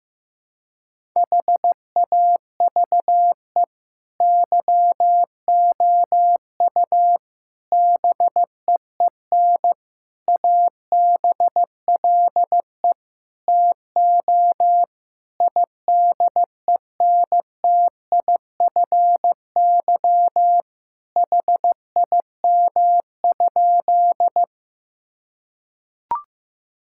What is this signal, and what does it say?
Morse code, heard as a keyed tone, about 15 wpm, 700 Hz. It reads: HAVE YOU BEEN ABLE TO IDENTIFY HIM?
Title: Sentences from Top 1000 Words - Review Part 6 - 15wpm